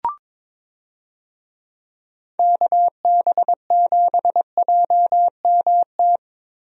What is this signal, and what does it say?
Morse code, heard as a keyed tone, about 22 wpm, 700 Hz. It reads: KB7JMT
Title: USA Call Signs - 6 Characters - Repeated 3x - 22wpm